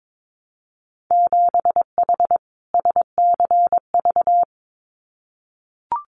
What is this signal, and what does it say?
Morse code, heard as a keyed tone, about 22 wpm, 700 Hz. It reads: 7H SCV
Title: Winter Field Day Exchanges Encore 22wpm